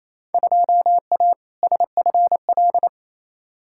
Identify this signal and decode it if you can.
Morse code, heard as a keyed tone, about 28 wpm, 700 Hz. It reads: 2A SFL
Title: Field Day Exchanges Encore - 28wpm